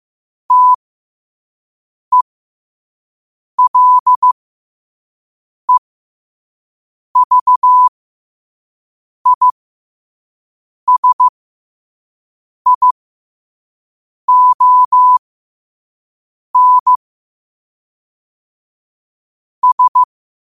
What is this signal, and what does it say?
Morse code, heard as a keyed tone, about 15 wpm, spaced as for 5 wpm, 1000 Hz. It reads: TELEVISION S
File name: qotdfiles / 5wpm